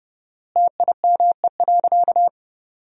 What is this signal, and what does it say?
Morse code, heard as a keyed tone, about 30 wpm, 700 Hz. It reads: TIME.
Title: 4x Word-Spacing - Sentences from Top 200 Words - 30wpm